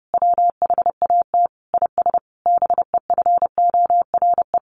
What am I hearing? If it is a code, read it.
Morse code, heard as a keyed tone, about 30 wpm, 700 Hz. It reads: WHAT IS BEFORE
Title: Sentences from Top 600 Words 30wpm